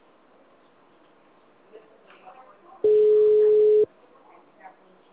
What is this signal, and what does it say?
Ringback tone (Europe, 425 Hz).
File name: out- -2010- - - .60493